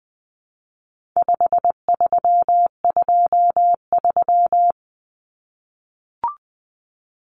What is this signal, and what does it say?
Morse code, heard as a keyed tone, about 20 wpm, 700 Hz. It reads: 5323
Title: Numbers - 4-Digit 20wpm